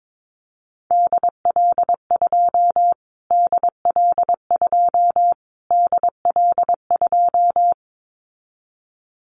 Morse code, heard as a keyed tone, about 22 wpm, 700 Hz. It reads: DL2 DL2 DL2
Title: Callsign Prefixes for Top 25 DXCC - Repeated 3x - 22wpm